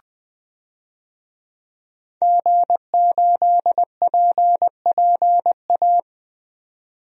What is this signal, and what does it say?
Morse code, heard as a keyed tone, about 20 wpm, 700 Hz. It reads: G8PPA
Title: DXCC-100 Call Signs - 5 Characters - 20wpm